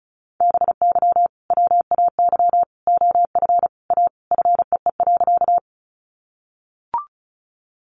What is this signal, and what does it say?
Morse code, heard as a keyed tone, about 34 wpm, 700 Hz. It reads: BY WAY OF A FEE.
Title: Sentences from Top 1500 Words 35wpm